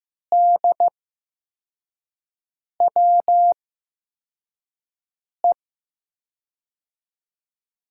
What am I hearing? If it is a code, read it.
Morse code, heard as a keyed tone, about 15 wpm, spaced as for 4 wpm, 700 Hz. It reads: DWE
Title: USA Call Signs - 5 Characters - 8x Character Spacing - 15wpm